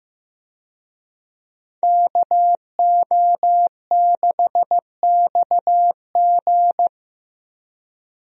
Morse code, heard as a keyed tone, about 15 wpm, 700 Hz. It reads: KO6XG
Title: USA Call Signs - 5 Characters - Repeated 3x - 15wpm